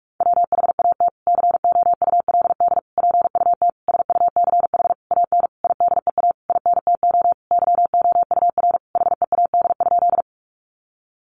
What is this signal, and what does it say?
Morse code, heard as a keyed tone, about 44 wpm, 700 Hz. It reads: WHAT COULD PUT SUCH AN IDEA INTO YOUR HEAD?
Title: Sentences from Top 600 Words 45wpm